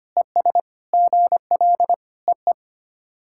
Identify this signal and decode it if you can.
Morse code, heard as a keyed tone, about 25 wpm, 690 Hz. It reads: ES GL EE
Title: SOTA QSO Practice 25wpm